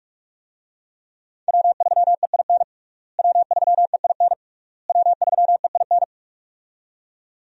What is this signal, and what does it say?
Morse code, heard as a keyed tone, about 47 wpm, 700 Hz. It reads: W3EIN W3EIN W3EIN
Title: USA Call Signs - 5 Characters - Repeated 3x - 45wpm